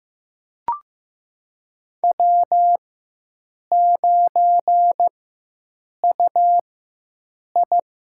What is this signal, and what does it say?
Morse code, heard as a keyed tone, about 16 wpm, spaced as for 7 wpm, 700 Hz. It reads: W9UI
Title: USA Call Signs - 4 Characters - 4x Character Spacing - 15wpm